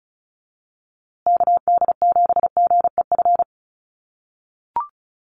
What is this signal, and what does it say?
Morse code, heard as a keyed tone, about 34 wpm, 700 Hz. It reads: KD7GEF
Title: USA Call Signs Repeated 3x 35wpm